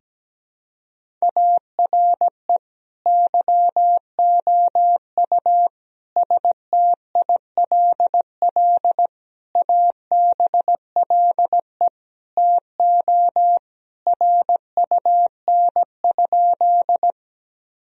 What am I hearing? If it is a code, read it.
Morse code, heard as a keyed tone, about 17 wpm, 700 Hz. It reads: ARE YOU STILL ABLE TO RUN?
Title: Sentences from Top 1000 Words - Review Part 2 - 17wpm